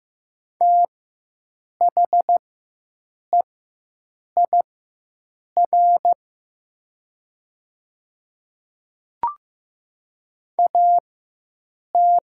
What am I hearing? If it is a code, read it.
Morse code, heard as a keyed tone, about 15 wpm, spaced as for 7 wpm, 700 Hz. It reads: THEIR AT
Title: 4x Character Spacing - Top 100 Words - 15wpm